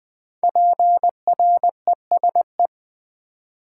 Morse code, heard as a keyed tone, about 20 wpm, 700 Hz. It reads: PRESE
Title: Build A Word - 5 Letters At A Time - 12-Letter Words - 20wpm